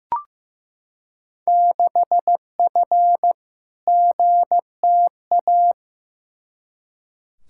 Morse code, heard as a keyed tone, about 15 wpm, 700 Hz. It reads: 6F GTA
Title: Field Day Exchanges Encore - 15wpm